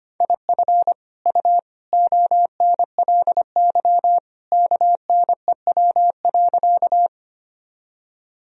Morse code, heard as a keyed tone, about 25 wpm, 700 Hz. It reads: IF U ONLY KNEW.
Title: Sentences from Top 400 Words and Abbreviations 25wpm